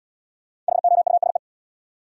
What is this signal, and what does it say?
Morse code, heard as a keyed tone, about 63 wpm, 700 Hz. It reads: HOUSE